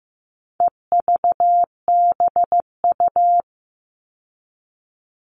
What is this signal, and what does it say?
Morse code, heard as a keyed tone, about 15 wpm, 700 Hz. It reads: EVBU
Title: All Cut Numbers - 4-Digit 15wpm